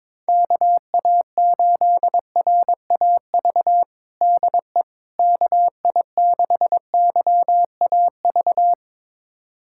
Morse code, heard as a keyed tone, about 22 wpm, 700 Hz. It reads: KA8RAV DE KI6YAV
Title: USA Callsign DE Callsign - 6 Characters - 22wpm